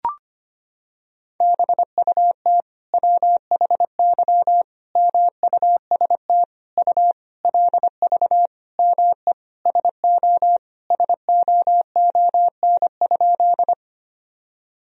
Morse code, heard as a keyed tone, about 25 wpm, 700 Hz. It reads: BUT WHY MUST U LV ME SO SOON?
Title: Sentences from Top 400 Words and Abbreviations 25wpm